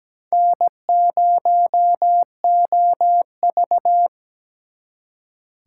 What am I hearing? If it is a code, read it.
Morse code, heard as a keyed tone, about 20 wpm, 700 Hz. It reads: N0OV
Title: Speed Racing - USA 1x2 Call Signs 17wpm